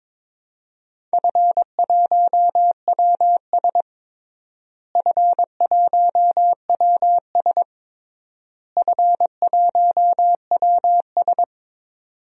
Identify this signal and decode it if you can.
Morse code, heard as a keyed tone, about 22 wpm, 700 Hz. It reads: F1WS F1WS F1WS
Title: DXCC-100 Call Signs - 4 Characters - Repeated 3x - 22wpm